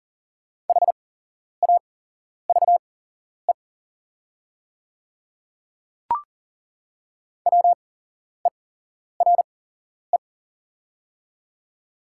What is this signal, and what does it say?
Morse code, heard as a keyed tone, about 40 wpm, spaced as for 11 wpm, 700 Hz. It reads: HAVE WERE
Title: 8x Character Spacing - Top 200 Words - 40wpm